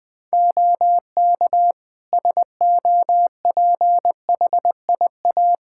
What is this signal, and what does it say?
Morse code, heard as a keyed tone, about 20 wpm, 700 Hz. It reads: OK SOPHIA
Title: Practice QSO - Exchange 3 - More Info - 20wpm